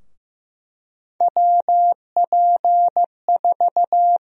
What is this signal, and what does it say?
Morse code, heard as a keyed tone, about 15 wpm, 700 Hz. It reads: WP4
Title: USA Call Sign Prefixes - Repeated 2x - 15wpm